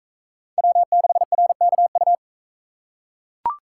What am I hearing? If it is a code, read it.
Morse code, heard as a keyed tone, about 42 wpm, 700 Hz. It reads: W6RKU